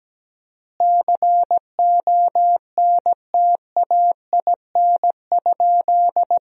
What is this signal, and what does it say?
Morse code, heard as a keyed tone, about 17 wpm, 700 Hz. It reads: CONTAIN?
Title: 4x Word-Spacing - Sentences from Top 500 Words - 17wpm